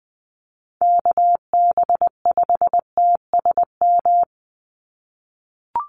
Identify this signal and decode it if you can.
Morse code, heard as a keyed tone, about 20 wpm, 700 Hz. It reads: KB5TSM
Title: USA Call Signs 20wpm